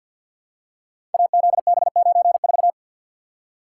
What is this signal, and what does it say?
Morse code, heard as a keyed tone, about 53 wpm, 700 Hz. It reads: AZB94
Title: Code Groups - 5 Letter:Numbers - 50wpm